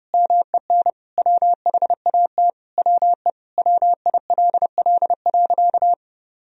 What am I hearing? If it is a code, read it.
Morse code, heard as a keyed tone, about 32 wpm, 700 Hz. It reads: MEN WHAT WE WILL.
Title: Sentences from Top 1000 Words - Review Part 1 - 30wpm